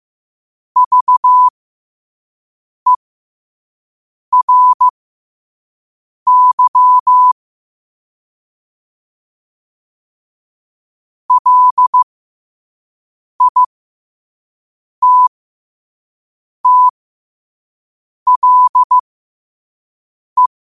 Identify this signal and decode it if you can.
Morse code, heard as a keyed tone, about 15 wpm, spaced as for 5 wpm, 1000 Hz. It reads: VERY LITTLE
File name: qotdfiles / 5wpm